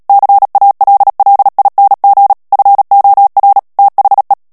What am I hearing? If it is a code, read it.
Morse code, heard as a keyed tone, about 37 wpm, 800 Hz. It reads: CALLING FOR THE
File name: morse mp3 / ANGLAIS texte clair 3 vitesse 35 mots minute